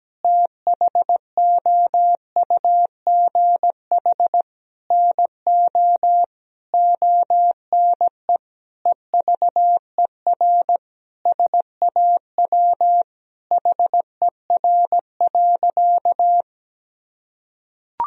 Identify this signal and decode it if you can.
Morse code, heard as a keyed tone, about 17 wpm, 700 Hz. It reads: THOUGH NO ONE EVER SAW HER.
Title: Sentences from Top 1000 Words - Review Part 2 - 17wpm